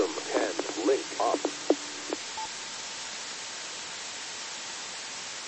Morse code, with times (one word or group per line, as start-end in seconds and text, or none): none